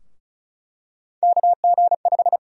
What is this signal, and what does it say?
Morse code, heard as a keyed tone, about 36 wpm, 700 Hz. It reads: KG5
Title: USA Call Sign Prefixes - 35wpm